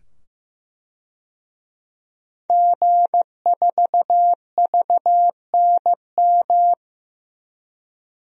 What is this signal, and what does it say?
Morse code, heard as a keyed tone, about 15 wpm, 700 Hz. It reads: G4VNM
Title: DXCC-100 Call Signs - 5 Characters - 15wpm